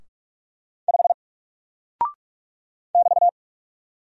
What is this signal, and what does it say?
Morse code, heard as a keyed tone, about 44 wpm, 700 Hz. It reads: 5 <BT>
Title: Single Character (Letter:Number:Punctuation:Prosign) Alt 45wpm